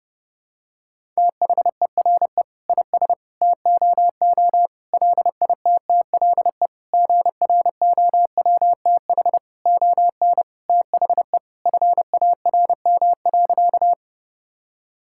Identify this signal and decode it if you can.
Morse code, heard as a keyed tone, about 31 wpm, 700 Hz. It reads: THERE IS TOO LITTLE GROWTH ON THE FARM.
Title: Sentences from Top 1000 Words - Review Part 5 - 30wpm